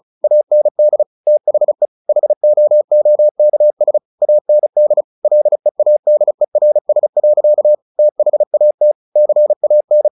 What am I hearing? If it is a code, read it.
Morse code, heard as a keyed tone, about 34 wpm, 600 Hz. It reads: AND THE HOOKS AND LEADERS. THAT CAN
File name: the old man and the sea / the old man and the sea 1 35wpm